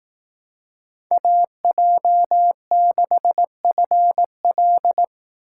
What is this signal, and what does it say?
Morse code, heard as a keyed tone, about 18 wpm, 700 Hz. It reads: AJ6FL
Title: Speed Racing - USA 2x2 Call Signs 15wpm